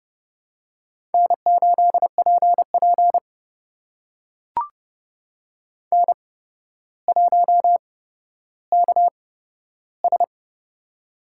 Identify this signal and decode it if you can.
Morse code, heard as a keyed tone, about 30 wpm, spaced as for 13 wpm, 700 Hz. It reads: N8PP N1KS
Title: USA Call Signs - 4 Characters - 8x Character Spacing - 30wpm